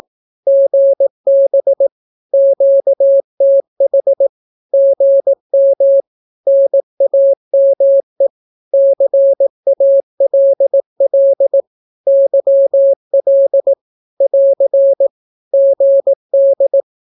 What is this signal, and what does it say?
Morse code, heard as a keyed tone, about 18 wpm, 550 Hz. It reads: GB QTH GM NAME CALL YL <AR> GD
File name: naukapodstaw / skroty / skroty